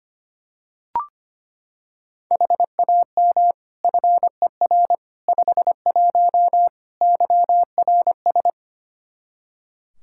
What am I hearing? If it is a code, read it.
Morse code, heard as a keyed tone, about 25 wpm, 700 Hz. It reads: HAM FER 51 YRS
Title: Practice QSO - Exchange 3 - More Info - 25wpm